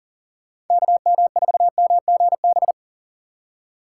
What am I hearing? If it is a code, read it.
Morse code, heard as a keyed tone, about 51 wpm, 700 Hz. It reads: KM4MGB